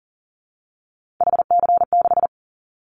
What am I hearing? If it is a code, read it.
Morse code, heard as a keyed tone, about 40 wpm, 700 Hz. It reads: HC6
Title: Callsign Prefixes for Top 100 DXCC - 40wpm